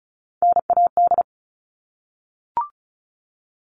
Morse code, heard as a keyed tone, about 34 wpm, 700 Hz. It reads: NAD
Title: All Cut Numbers - 3-Digit 35wpm